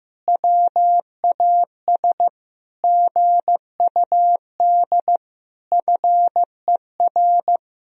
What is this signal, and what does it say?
Morse code, heard as a keyed tone, about 16 wpm, 700 Hz. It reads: WAS GUD FER
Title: Sentences from Top 400 Words and Abbreviations 15wpm